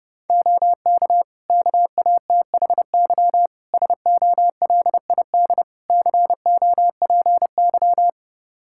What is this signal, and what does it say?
Morse code, heard as a keyed tone, about 30 wpm, 700 Hz. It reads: OK KATHY SOLID COPY
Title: Practice QSO - Exchange 3 - More Info - 30wpm